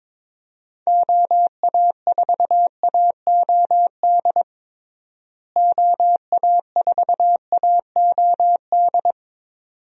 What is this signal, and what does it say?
Morse code, heard as a keyed tone, about 22 wpm, 700 Hz. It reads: OA4AOD OA4AOD
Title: USA Call Signs - 6 Characters - Repeated 2x - 22wpm